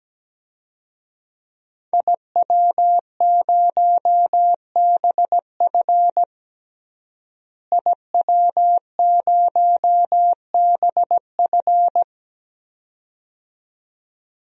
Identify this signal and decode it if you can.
Morse code, heard as a keyed tone, about 17 wpm, 700 Hz. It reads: IW0BF IW0BF
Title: DXCC-100 Call Signs - 5 Characters - Repeated 2x - 17wpm